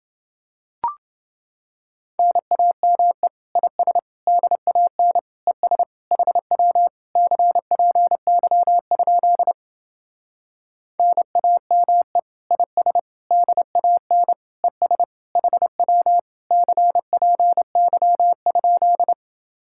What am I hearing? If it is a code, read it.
Morse code, heard as a keyed tone, about 30 wpm, 700 Hz. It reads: NAME IS DAN ES HW CPY? NAME IS DAN ES HW CPY?